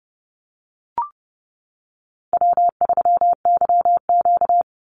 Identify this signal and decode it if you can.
Morse code, heard as a keyed tone, about 28 wpm, 700 Hz. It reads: W3YQ
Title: Easy Call Signs Repeated 3x - 30wpm